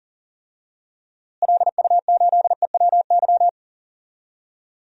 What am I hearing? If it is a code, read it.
Morse code, heard as a keyed tone, about 40 wpm, 700 Hz. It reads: LU8EWY
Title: DXCC-100 Call Signs - 6 Characters - Repeated 2x - 40wpm